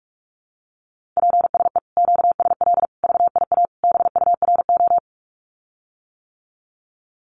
Morse code, heard as a keyed tone, about 44 wpm, 700 Hz. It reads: PSE QSL VIA BURO
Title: Common QSO Phrases Encore 45wpm